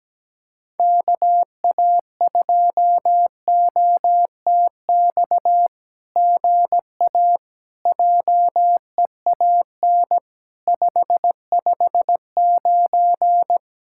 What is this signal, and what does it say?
Morse code, heard as a keyed tone, about 17 wpm, 700 Hz. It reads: KA2OTX GA JEAN 559
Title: SOTA QSO Practice 17wpm